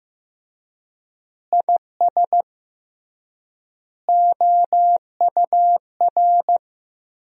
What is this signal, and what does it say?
Morse code, heard as a keyed tone, about 15 wpm, 700 Hz. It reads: IS OUR
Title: 3x Word-Spacing - Sentences from Top 500 Words - 15wpm